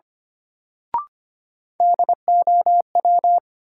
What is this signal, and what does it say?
Morse code, heard as a keyed tone, about 25 wpm, 700 Hz. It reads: DOW
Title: Sets of 5 Words - New Word Drill - 25wpm